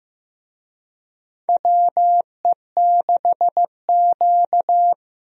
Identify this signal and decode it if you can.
Morse code, heard as a keyed tone, about 15 wpm, 700 Hz. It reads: WE6Q
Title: USA Call Signs - 4 Characters - Repeated 2x - 15wpm